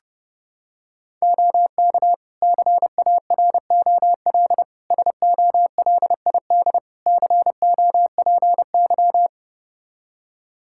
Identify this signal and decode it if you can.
Morse code, heard as a keyed tone, about 32 wpm, 700 Hz. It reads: OK CAROL SOLID COPY
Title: K1USN SST QSO Practice 30wpm